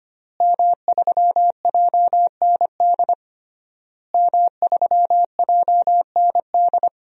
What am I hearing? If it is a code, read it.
Morse code, heard as a keyed tone, about 25 wpm, 700 Hz. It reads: M3JND M3JND
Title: DXCC-100 Call Signs - 5 Characters - Repeated 3x - 25wpm